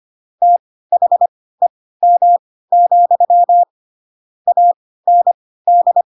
Morse code, heard as a keyed tone, about 25 wpm, spaced as for 16 wpm, 700 Hz. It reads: THEM, AND
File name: Pinocchio / Pinocchio ch3 10 25 ik1hge